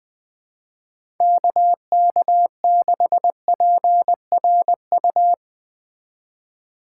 Morse code, heard as a keyed tone, about 22 wpm, 700 Hz. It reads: KK6PRU
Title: Speed Racing - USA 2x3 Call Signs 20wpm